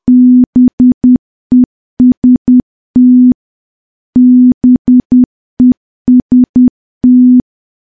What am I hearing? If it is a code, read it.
Morse code, heard as a keyed tone, about 10 wpm, 250 Hz. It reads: BEST BEST
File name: morse / morse-training-10WPM